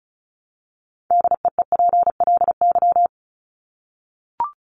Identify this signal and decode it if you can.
Morse code, heard as a keyed tone, about 34 wpm, 700 Hz. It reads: DEEPLY
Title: Top 1751-2000 Words 35wpm